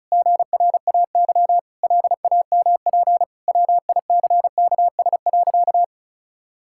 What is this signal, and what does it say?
Morse code, heard as a keyed tone, about 34 wpm, 700 Hz. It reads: GRAY LAMP WICKS.
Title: Pangram Challenge v2 - 35wpm